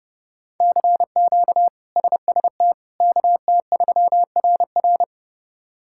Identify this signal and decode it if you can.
Morse code, heard as a keyed tone, about 32 wpm, 700 Hz. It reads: CQ SST KT3RR
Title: K1USN SST QSO Practice - 30wpm